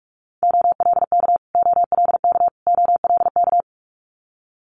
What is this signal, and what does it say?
Morse code, heard as a keyed tone, about 46 wpm, 690 Hz. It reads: OLK OLK OLK
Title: Multi-letter Phonemes Repeated 3x 45wpm